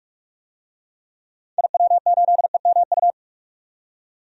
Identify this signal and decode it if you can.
Morse code, heard as a keyed tone, about 43 wpm, 700 Hz. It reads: IW8EMU
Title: DXCC-100 Call Signs - 6 Characters - 45wpm